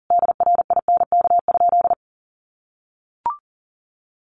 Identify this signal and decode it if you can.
Morse code, heard as a keyed tone, about 40 wpm, 700 Hz. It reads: DRINK?
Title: Sentences from Top 1500 Words 40wpm